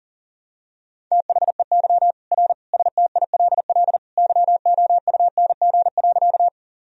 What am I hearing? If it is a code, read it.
Morse code, heard as a keyed tone, about 40 wpm, 700 Hz. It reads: THEY R STILL YOUNG.
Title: Sentences from Top 400 Words and Abbreviations 40wpm